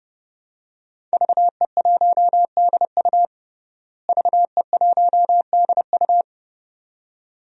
Morse code, heard as a keyed tone, about 31 wpm, 700 Hz. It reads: VE1DU VE1DU
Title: DXCC-100 Call Signs - 5 Characters - Repeated 2x - 30wpm